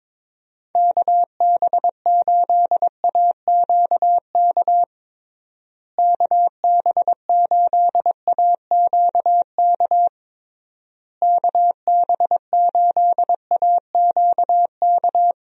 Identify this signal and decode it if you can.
Morse code, heard as a keyed tone, about 21 wpm, 700 Hz. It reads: KB8AQK KB8AQK KB8AQK
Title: USA Call Signs - 6 Characters - Repeated 3x - 22wpm